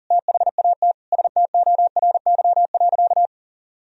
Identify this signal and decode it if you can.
Morse code, heard as a keyed tone, about 40 wpm, 700 Hz. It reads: THAT STORY.
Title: Sentences from Top 1000 Words - Review Part 4 - 40wpm